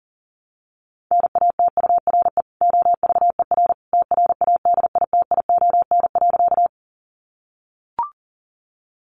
Morse code, heard as a keyed tone, about 40 wpm, 700 Hz. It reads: NATURE OVER TRADITION.
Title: Sentences from Top 1500 Words 40wpm